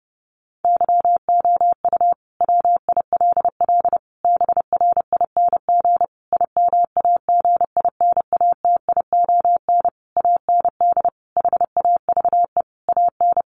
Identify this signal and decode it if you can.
Morse code, heard as a keyed tone, about 30 wpm, 700 Hz. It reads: YOU WILL BRING IMAGINATION AND HAVE AN
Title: Sentences from Top 1300 Words 30wpm